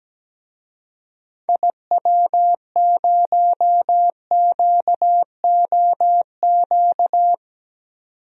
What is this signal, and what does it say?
Morse code, heard as a keyed tone, about 17 wpm, 700 Hz. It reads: IW0QOQ
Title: DXCC-100 Call Signs - 6 Characters - 17wpm